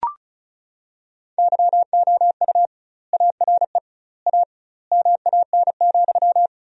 Morse code, heard as a keyed tone, about 36 wpm, 700 Hz. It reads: YOU ARE A MAN,
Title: 2x Word-Spacing - Sentences from Top 200 Words - 35wpm